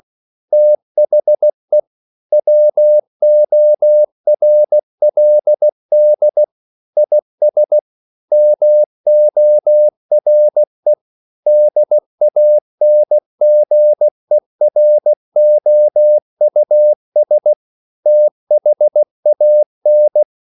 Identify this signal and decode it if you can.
Morse code, heard as a keyed tone, about 16 wpm, 600 Hz. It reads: THE WORLD IS MORE DANGEROUS THAN
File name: CW / Practice / EnglishQuotations0116